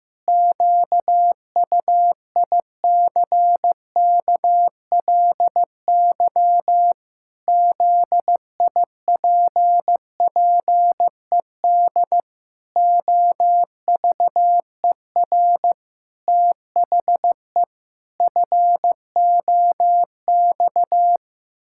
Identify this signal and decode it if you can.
Morse code, heard as a keyed tone, about 15 wpm, 700 Hz. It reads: QUICKLY ZIPPED OVER THE FOX
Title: Pangram Challenge v2 - 15wpm